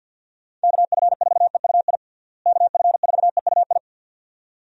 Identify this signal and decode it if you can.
Morse code, heard as a keyed tone, about 49 wpm, 700 Hz. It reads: KF4EUI KF4EUI
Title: USA Call Signs - 6 Characters - Repeated 2x - 50wpm